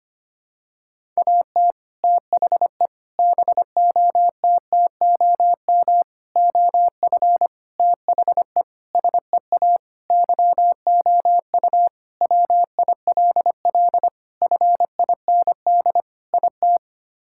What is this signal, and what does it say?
Morse code, heard as a keyed tone, about 25 wpm, 700 Hz. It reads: AT THE BOTTOM OF THE SEA YOU WILL FIND IT
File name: practice / 25wpm / Sentences with 3-Word Phrases - 25wpm